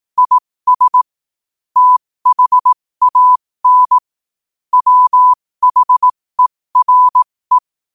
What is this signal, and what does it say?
Morse code, heard as a keyed tone, about 18 wpm, spaced as for 15 wpm, 1000 Hz. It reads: IS THAN WHERE
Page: qotd-at-15-wpm